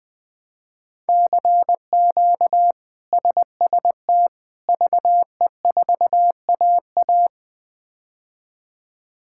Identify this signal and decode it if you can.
Morse code, heard as a keyed tone, about 20 wpm, 700 Hz. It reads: CQ SST VE4AA
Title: K1USN SST QSO Practice - 20wpm